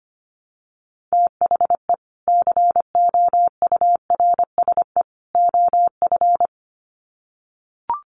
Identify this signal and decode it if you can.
Morse code, heard as a keyed tone, about 25 wpm, 700 Hz. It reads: THE COURSE OF
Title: Sets of 3 Words 25wpm